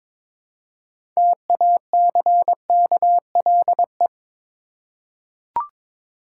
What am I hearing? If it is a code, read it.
Morse code, heard as a keyed tone, about 22 wpm, 700 Hz. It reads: TACKLE
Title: Six Letter Words - 22wpm